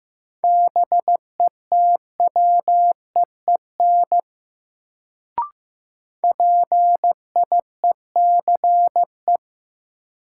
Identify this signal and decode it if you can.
Morse code, heard as a keyed tone, about 15 wpm, 700 Hz. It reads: BETWEEN PIECE
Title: Sentences from Top 400 Words - New Word Drill - 15wpm